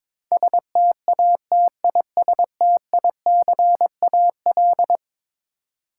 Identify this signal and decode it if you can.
Morse code, heard as a keyed tone, about 22 wpm, 700 Hz. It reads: STATISTICAL
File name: practice / 22wpm / Eleven Letter Words - 22wpm